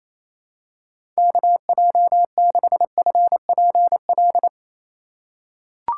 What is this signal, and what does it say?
Morse code, heard as a keyed tone, about 28 wpm, 700 Hz. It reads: KJ6FPL
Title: USA Call Signs - 6 Characters - 4x Character Spacing - 28wpm